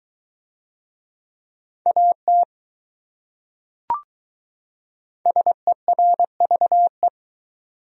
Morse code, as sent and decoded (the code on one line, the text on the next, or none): .- - / ... .-. ...- .
AT SERVE